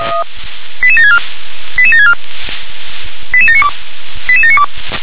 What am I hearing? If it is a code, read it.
Five-tone selective call (ZVEI), heard as quick runs of five tones, tones of about 70 ms. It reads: 88764 88764 88792 88792